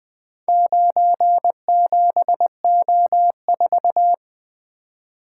Morse code, heard as a keyed tone, about 20 wpm, 700 Hz. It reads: 97O4
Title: Four Letter-Number Combinations - 20wpm